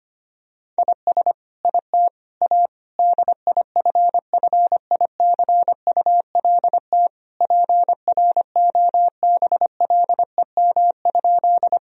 Morse code, heard as a keyed tone, about 25 wpm, 700 Hz. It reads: IS IT A DIFFICULT PROBLEM?
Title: Sentences from Top 1000 Words - Review Part 2 - 25wpm